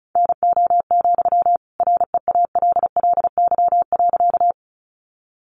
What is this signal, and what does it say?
Morse code, heard as a keyed tone, about 34 wpm, 700 Hz. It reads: NO, REALLY.
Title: Sentences from Top 600 Words 35wpm